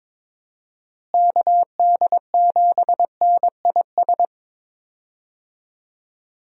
Morse code, heard as a keyed tone, about 22 wpm, 700 Hz. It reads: KD7NIS
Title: USA Call Signs - 6 Characters - Repeated 2x - 22wpm